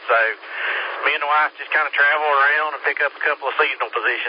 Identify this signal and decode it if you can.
Emergency Alert System (SAME) tones, heard as harsh buzzing bursts, with speech. It is heard from 0.2 to 3.4 s.